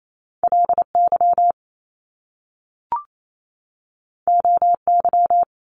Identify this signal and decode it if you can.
Morse code, heard as a keyed tone, about 28 wpm, 700 Hz. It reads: LY OY
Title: Top Two Letter Combinations in Words 28wpm